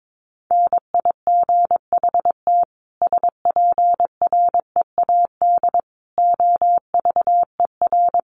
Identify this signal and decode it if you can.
Morse code, heard as a keyed tone, about 22 wpm, 700 Hz. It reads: NIGHT SPREAD OVER
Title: Sentences from Top 1200 Words 22wpm